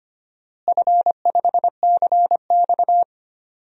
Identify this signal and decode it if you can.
Morse code, heard as a keyed tone, about 25 wpm, 700 Hz. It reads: F5CX
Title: DXCC-100 Call Signs - 4 Characters - Repeated 2x - 25wpm